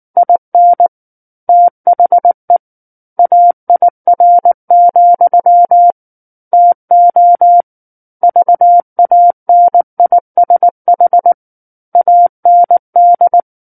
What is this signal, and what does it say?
Morse code, heard as a keyed tone, about 19 wpm, 700 Hz. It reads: IN THE AIR, TO VANISH AND
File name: Book One / 10-Chapter 10 - 19 WPM